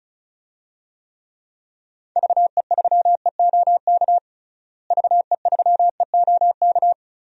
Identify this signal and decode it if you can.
Morse code, heard as a keyed tone, about 36 wpm, 700 Hz. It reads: VE3EOK VE3EOK
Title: DXCC-100 Call Signs - 6 Characters - Repeated 3x - 35wpm